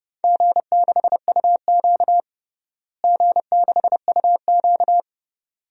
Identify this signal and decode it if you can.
Morse code, heard as a keyed tone, about 30 wpm, 700 Hz. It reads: G6UQ G6UQ
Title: DXCC-100 Call Signs - 4 Characters - Repeated 2x - 30wpm